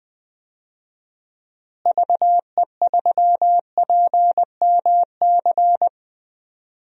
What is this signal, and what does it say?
Morse code, heard as a keyed tone, about 20 wpm, 700 Hz. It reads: VE3PMC
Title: DXCC-100 Call Signs - 6 Characters - Repeated 3x - 20wpm